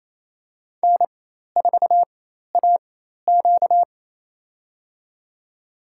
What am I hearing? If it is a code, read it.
Morse code, heard as a keyed tone, about 28 wpm, spaced as for 13 wpm, 700 Hz. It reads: N4AQ
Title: USA Call Signs - 4 Characters - 4x Character Spacing - 28wpm